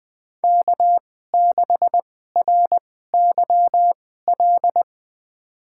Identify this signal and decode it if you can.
Morse code, heard as a keyed tone, about 21 wpm, spaced as for 15 wpm, 700 Hz. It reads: K6RYL
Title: USA Call Signs - 2x Character Spacing - 20wpm